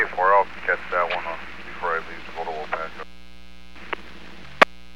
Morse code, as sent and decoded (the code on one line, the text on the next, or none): none